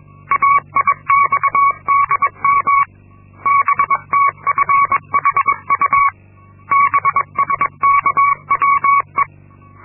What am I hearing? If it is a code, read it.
Morse code, heard as a keyed tone, about 22 wpm, 1120 Hz. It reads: AIXDM BTFHU BSKWE
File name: Linksmakalnis-2